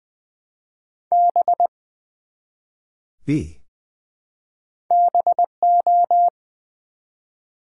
Morse code, heard as a keyed tone, about 20 wpm, 700 Hz. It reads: B BO
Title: Build A Word - 1 Letter At A Time - 5-Letter Words - 20wpm